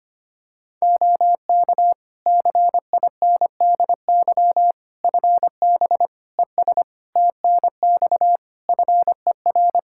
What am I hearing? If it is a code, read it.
Morse code, heard as a keyed tone, about 25 wpm, 700 Hz. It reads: OK CINDY FB ES TNX FER
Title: K1USN SST QSO Practice 25wpm